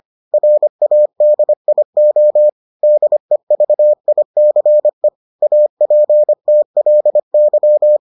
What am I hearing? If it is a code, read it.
Morse code, heard as a keyed tone, about 25 wpm, 600 Hz. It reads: RADIO DEVICE APTLY